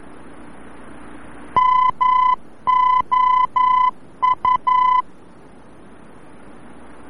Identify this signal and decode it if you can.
Morse code, heard as a keyed tone, about 11 wpm, 1010 Hz. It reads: MOU
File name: france / mx / MOU-VOR-F